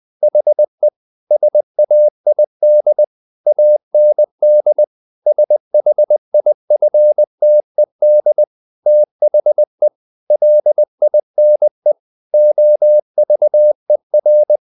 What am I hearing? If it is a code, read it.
Morse code, heard as a keyed tone, about 20 wpm, 600 Hz. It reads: HE SAID AND SHIFTED THE LINE OVER
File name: the old man and the sea / the old man and the sea part 1 20wpm